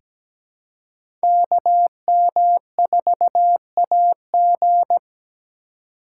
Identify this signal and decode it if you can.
Morse code, heard as a keyed tone, about 17 wpm, 700 Hz. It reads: KM4AG
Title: USA Call Signs - 5 Characters - 2x Character Spacing - 17wpm